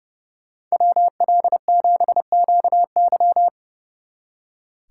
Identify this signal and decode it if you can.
Morse code, heard as a keyed tone, about 39 wpm, 700 Hz. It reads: WL7QY